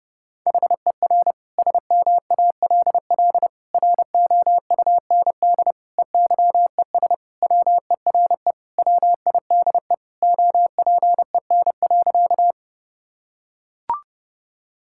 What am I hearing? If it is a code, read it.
Morse code, heard as a keyed tone, about 30 wpm, 700 Hz. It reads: HER SMALL ROUND EYES WERE WIDE OPEN.
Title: Sentences from Top 1000 Words - Review Part 6 - 30wpm